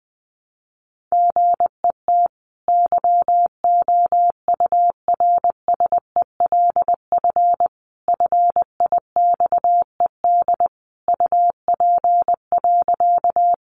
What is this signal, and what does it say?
Morse code, heard as a keyed tone, about 19 wpm, 700 Hz. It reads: GET YOURSELF FIXED UP.
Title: Sentences from Top 1100 Words 20wpm